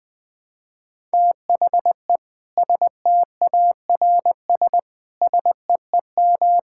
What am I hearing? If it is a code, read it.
Morse code, heard as a keyed tone, about 21 wpm, 700 Hz. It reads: THE STARS SEEM
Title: Difficult Sentences with Hints - 20wpm